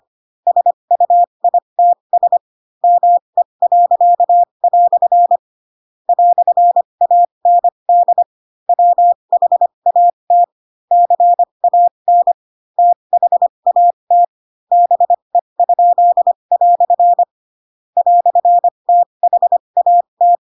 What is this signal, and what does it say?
Morse code, heard as a keyed tone, about 25 wpm, spaced as for 22 wpm, 700 Hz. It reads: SUITS ME." "AND WHAT CAN THAT BE?" "THAT
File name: Pinocchio / Pinocchio ch4 18 25 ik1hge